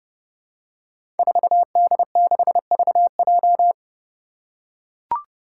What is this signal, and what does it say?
Morse code, heard as a keyed tone, about 30 wpm, 700 Hz. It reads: 4D6VJ
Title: Five Challenging Letter-Number Combinations 30wpm